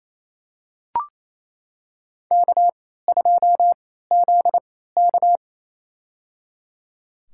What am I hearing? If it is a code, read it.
Morse code, heard as a keyed tone, about 27 wpm, spaced as for 16 wpm, 700 Hz. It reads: K2ZK
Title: USA Call Signs - 4 Characters - 3x Character Spacing - 28wpm